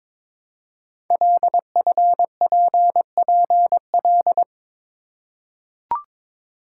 Morse code, heard as a keyed tone, about 22 wpm, 700 Hz. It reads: LFPPL
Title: Code Groups - 5 Letters - 22wpm